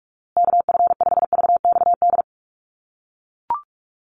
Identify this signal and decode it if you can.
Morse code, heard as a keyed tone, about 44 wpm, 700 Hz. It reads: KF5VXD
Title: USA Call Signs Repeated 3x 45wpm